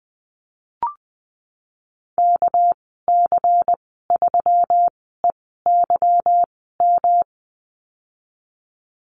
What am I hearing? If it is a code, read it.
Morse code, heard as a keyed tone, about 20 wpm, spaced as for 15 wpm, 700 Hz. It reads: KC3EYM
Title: 238 Lesson 38g - Letter q - Call Signs 20wpm